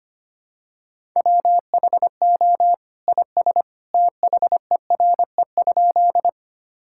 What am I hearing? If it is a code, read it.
Morse code, heard as a keyed tone, about 25 wpm, 700 Hz. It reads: WHO IS THERE?
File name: practice / 25wpm / Sentences with 2-Word Phrases - 25wpm